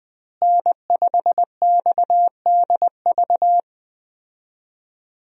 Morse code, heard as a keyed tone, about 20 wpm, 700 Hz. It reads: N5XDV
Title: USA Call Signs - 5 Characters - Repeated 2x - 20wpm